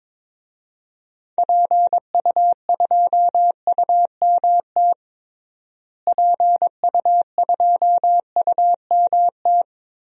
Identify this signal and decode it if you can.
Morse code, heard as a keyed tone, about 22 wpm, 700 Hz. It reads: PU2UMT PU2UMT
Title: DXCC-100 Call Signs - 6 Characters - Repeated 3x - 22wpm